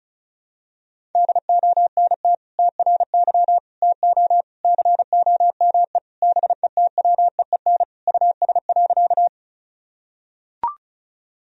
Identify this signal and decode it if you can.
Morse code, heard as a keyed tone, about 36 wpm, 700 Hz. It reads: DONT TRY TO COME BETWEEN US.
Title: Sentences from Top 1000 Words - Review Part 1 - 35wpm